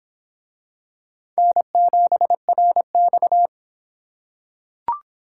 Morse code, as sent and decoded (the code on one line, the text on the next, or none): -. --... .-. -..-
N7RX